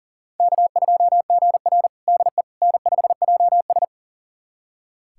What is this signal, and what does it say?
Morse code, heard as a keyed tone, about 40 wpm, 700 Hz. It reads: K2GR DE N5JS
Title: USA Callsign DE Callsign - 4 Characters - 40wpm